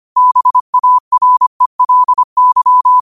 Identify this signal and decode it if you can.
Morse code, heard as a keyed tone, about 25 wpm, 1000 Hz. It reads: BARELY
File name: qotdfiles / 25wpm